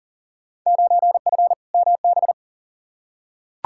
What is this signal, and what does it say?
Morse code, heard as a keyed tone, about 40 wpm, 700 Hz. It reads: 9F MB
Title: Field Day Exchanges Encore - 40wpm